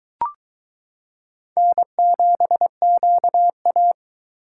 Morse code, heard as a keyed tone, about 22 wpm, 700 Hz. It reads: N7QA